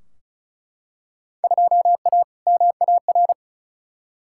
Morse code, heard as a keyed tone, about 34 wpm, 700 Hz. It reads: 2A MAR
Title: Field Day Exchanges - 35wpm